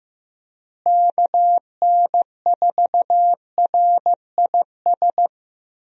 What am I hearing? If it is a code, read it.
Morse code, heard as a keyed tone, about 15 wpm, 700 Hz. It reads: KN4RIS
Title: USA Call Signs - 6 Characters - Repeated 3x - 15wpm